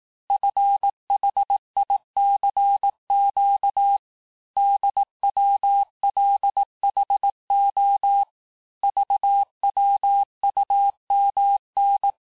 Morse code, heard as a keyed tone, about 18 wpm, 800 Hz. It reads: FHICQ DWLHO VWUMN
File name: cours-du-lundi files / cours-1852-texte-chiffre-vitesse-17MPM